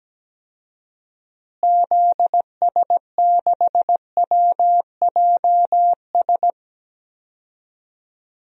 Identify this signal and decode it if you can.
Morse code, heard as a keyed tone, about 17 wpm, 700 Hz. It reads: ZS6WJS
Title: DXCC-100 Call Signs - 6 Characters - Repeated 3x - 17wpm